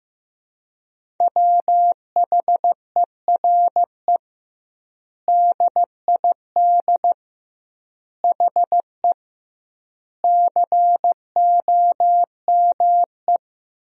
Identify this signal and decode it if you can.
Morse code, heard as a keyed tone, about 15 wpm, 700 Hz. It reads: WHERE DID HE COME
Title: 2x Word-Spacing - Sentences from Top 200 Words - 15wpm